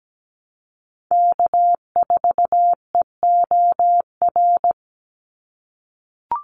Morse code, heard as a keyed tone, about 17 wpm, 700 Hz. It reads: K4EOR
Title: Call Signs - 17wpm